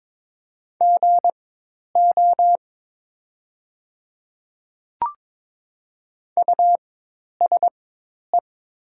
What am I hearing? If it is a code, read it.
Morse code, heard as a keyed tone, about 22 wpm, spaced as for 10 wpm, 690 Hz. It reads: GO USE
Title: 4x Character Spacing - Top 100 Words - 22wpm